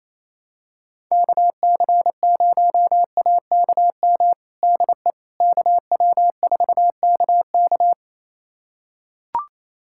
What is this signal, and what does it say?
Morse code, heard as a keyed tone, about 28 wpm, 700 Hz. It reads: KC0AKM DE KW4KK
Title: USA Callsign DE Callsign 28wpm